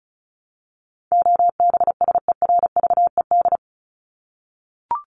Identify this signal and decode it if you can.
Morse code, heard as a keyed tone, about 34 wpm, 700 Hz. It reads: OBSERVED
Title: ED Suffix - Top 100 Words with Base Words 35wpm